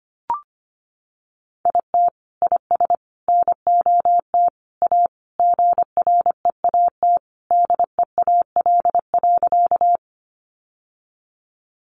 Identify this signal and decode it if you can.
Morse code, heard as a keyed tone, about 25 wpm, 700 Hz. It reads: IT IS NOT A GREAT DEAL.
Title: Sentences from Top 800 Words 25wpm